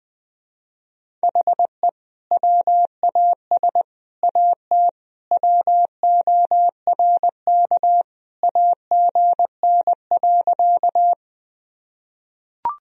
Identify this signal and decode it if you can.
Morse code, heard as a keyed tone, about 20 wpm, 700 Hz. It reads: HE WAS AT WORK AGN.
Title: Sentences from Top 200 Words and Abbreviations 20wpm